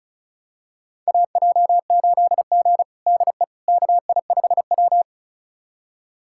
Morse code, heard as a keyed tone, about 36 wpm, 700 Hz. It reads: AJ8G DE KI5W
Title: USA Callsign DE Callsign - 4 Characters - 35wpm